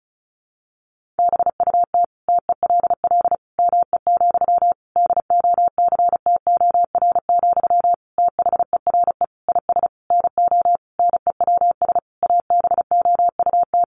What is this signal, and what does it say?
Morse code, heard as a keyed tone, about 34 wpm, 700 Hz. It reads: BUT TELL ME, DOCTOR, THERE IS NO NEWS ABOUT
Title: Sentences from Top 1500 Words 35wpm